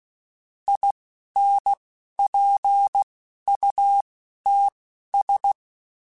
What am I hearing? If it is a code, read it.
Morse code, heard as a keyed tone, about 16 wpm, spaced as for 11 wpm, 800 Hz. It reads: INPUTS
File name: Technical Subjects / Technical Subjects-mp3s / OR Gate @10